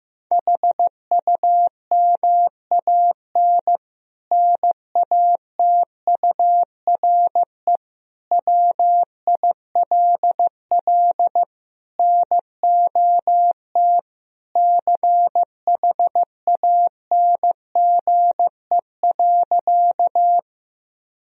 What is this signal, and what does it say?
Morse code, heard as a keyed tone, about 15 wpm, 700 Hz. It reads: HUMAN NATURE WILL NOT CHANGE.
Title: Sentences from Top 1000 Words - Review Part 3 - 15wpm